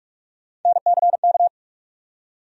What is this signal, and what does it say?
Morse code, heard as a keyed tone, about 42 wpm, 700 Hz. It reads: NCK
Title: USA Call Sign Postfixes - 3 Letters 45wpm